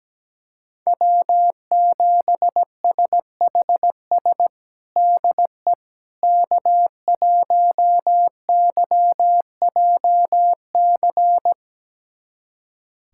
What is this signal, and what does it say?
Morse code, heard as a keyed tone, about 17 wpm, 700 Hz. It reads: W7SHS DE K1YJC
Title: USA Callsign DE Callsign - 5 Characters - 17wpm